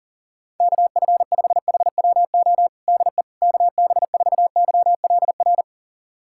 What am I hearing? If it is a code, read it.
Morse code, heard as a keyed tone, about 40 wpm, 700 Hz. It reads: KF5HWO DE KB4YLR
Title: USA Callsign DE Callsign - 6 Characters - 40wpm